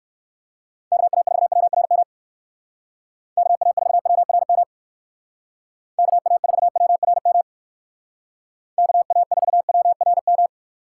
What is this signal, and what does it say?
Morse code, heard as a keyed tone, about 59 wpm, 700 Hz. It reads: KA4WRM KA4WRM KA4WRM KA4WRM